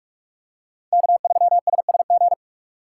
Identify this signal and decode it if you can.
Morse code, heard as a keyed tone, about 42 wpm, 700 Hz. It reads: K3SSG